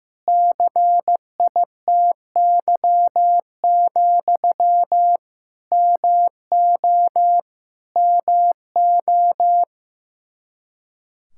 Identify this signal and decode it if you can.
Morse code, heard as a keyed tone, about 15 wpm, 700 Hz. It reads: CITY, MO MO
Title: K1USN SST QSO Practice 15wpm